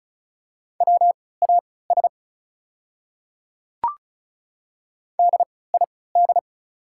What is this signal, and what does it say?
Morse code, heard as a keyed tone, about 36 wpm, spaced as for 20 wpm, 690 Hz. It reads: WAS DID
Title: 3x Character Spacing - Top 100 Words - 35wpm